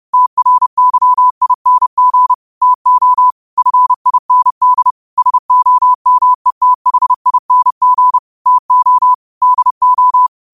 Morse code, heard as a keyed tone, about 30 wpm, 1000 Hz. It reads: TRYING TO FIND SOMETHING TO DO
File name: qotdfiles / 30wpm